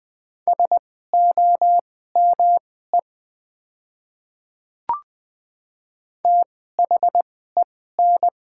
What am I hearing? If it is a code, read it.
Morse code, heard as a keyed tone, about 20 wpm, spaced as for 15 wpm, 700 Hz. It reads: SOME THEN
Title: 2x Character Spacing - Top 100 Words - 20wpm